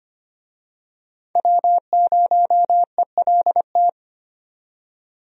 Morse code, heard as a keyed tone, about 25 wpm, 700 Hz. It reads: W0ELT
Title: USA Call Signs - 5 Characters - 25wpm